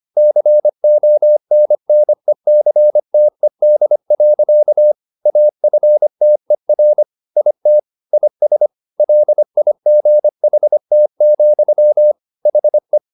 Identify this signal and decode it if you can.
Morse code, heard as a keyed tone, about 25 wpm, 600 Hz. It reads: CONNECTED. AFTER IT IS LIGHT, HE
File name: the old man and the sea part 1 25wpm